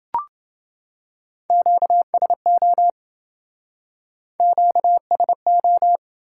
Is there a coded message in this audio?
Yes: Morse code from 1.5 to 6.0 s, heard as a keyed tone, about 27 wpm, 700 Hz. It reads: QSO QSO